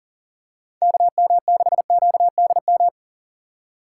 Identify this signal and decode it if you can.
Morse code, heard as a keyed tone, about 38 wpm, 700 Hz. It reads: KM6QDM
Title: USA Call Signs - 6 Characters - 3x Character Spacing - 40wpm